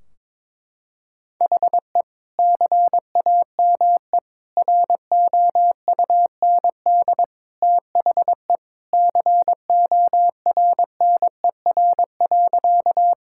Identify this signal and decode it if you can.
Morse code, heard as a keyed tone, about 22 wpm, 700 Hz. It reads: HE CAME ROUND THE CORNER.
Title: Sentences from Top 1600 Words 22wpm